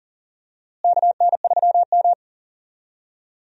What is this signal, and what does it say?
Morse code, heard as a keyed tone, about 40 wpm, 700 Hz. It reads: KN3M
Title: USA Call Signs - 4 Characters - Repeated 2x - 40wpm